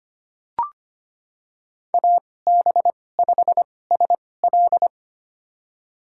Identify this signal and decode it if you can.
Morse code, heard as a keyed tone, about 25 wpm, spaced as for 18 wpm, 700 Hz. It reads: AB5SL
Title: USA Call Signs - 5 Characters - 2x Character Spacing - 25wpm